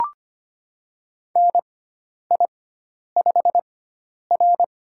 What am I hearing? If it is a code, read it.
Morse code, heard as a keyed tone, about 25 wpm, spaced as for 10 wpm, 700 Hz. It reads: NI5R
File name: Lesson 16g - Letter c - Call Signs 25wpm